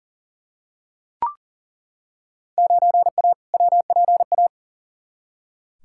Morse code, heard as a keyed tone, about 40 wpm, 700 Hz. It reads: 9A WPA
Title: Field Day Exchanges Encore - 40wpm